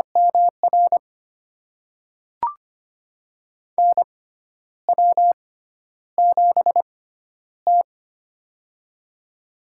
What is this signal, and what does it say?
Morse code, heard as a keyed tone, about 25 wpm, spaced as for 10 wpm, 700 Hz. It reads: MR NW7T
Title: USA Call Signs - 4 Characters - 6x Character Spacing - 25wpm